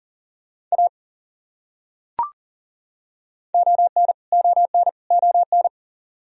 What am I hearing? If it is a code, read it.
Morse code, heard as a keyed tone, about 40 wpm, 700 Hz. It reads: A ON ON ON
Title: Top 100 Words Repeated 40wpm